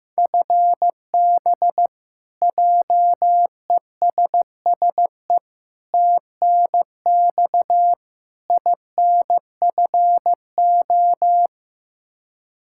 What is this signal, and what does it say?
Morse code, heard as a keyed tone, about 15 wpm, 700 Hz. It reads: FB JESSE TNX INFO
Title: Practice QSO - Exchange 3 - More Info - 15wpm